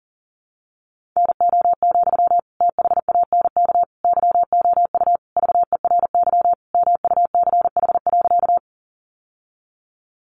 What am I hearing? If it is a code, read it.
Morse code, heard as a keyed tone, about 40 wpm, 700 Hz. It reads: NO, THANK YOU VERY MUCH.
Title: Sentences from Top 600 Words 40wpm